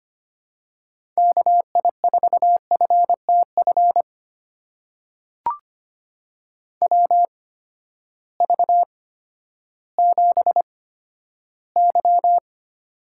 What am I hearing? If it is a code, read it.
Morse code, heard as a keyed tone, about 25 wpm, spaced as for 12 wpm, 700 Hz. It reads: KI4FTF WV7Y
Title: USA Call Signs - 8x Character Spacing - 25wpm